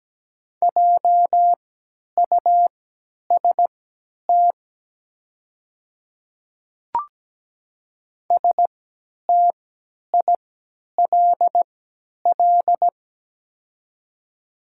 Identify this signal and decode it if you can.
Morse code, heard as a keyed tone, about 17 wpm, spaced as for 10 wpm, 690 Hz. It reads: JUST STILL
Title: 3x Character Spacing - Top 300 Words - 17wpm